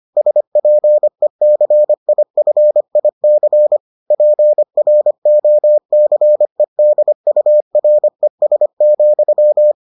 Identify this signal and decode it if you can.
Morse code, heard as a keyed tone, about 25 wpm, 600 Hz. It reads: SPECIFIC PROCEDURES,